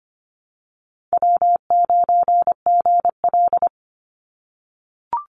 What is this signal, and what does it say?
Morse code, heard as a keyed tone, about 25 wpm, 700 Hz. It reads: W9GL
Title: Call Signs - 25wpm